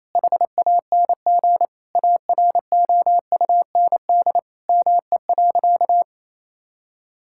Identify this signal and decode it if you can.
Morse code, heard as a keyed tone, about 27 wpm, 700 Hz. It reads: HANG AROUND ME.
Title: Sentences from Top 1600 Words 28wpm